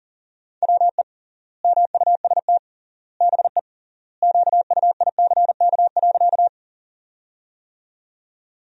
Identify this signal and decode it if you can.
Morse code, heard as a keyed tone, about 40 wpm, 700 Hz. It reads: WE MUST BE QUICK.
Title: 3x Word-Spacing - Sentences from Top 500 Words - 40wpm